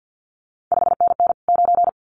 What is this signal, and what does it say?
Morse code, heard as a keyed tone, about 50 wpm, 700 Hz. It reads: 5NN 9
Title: CQ WW DX Contest Exchange - 50wpm